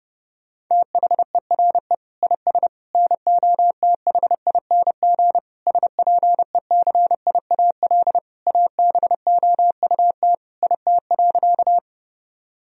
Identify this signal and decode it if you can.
Morse code, heard as a keyed tone, about 30 wpm, 700 Hz. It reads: THERE IS NOTHING SPECIAL ABOUT IT.
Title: Sentences from Top 1000 Words - Review Part 3 - 30wpm